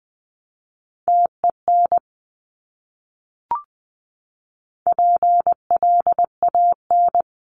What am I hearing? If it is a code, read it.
Morse code, heard as a keyed tone, about 20 wpm, 700 Hz. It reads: TEN PLAN
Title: Top 500 Words Repeated 20wpm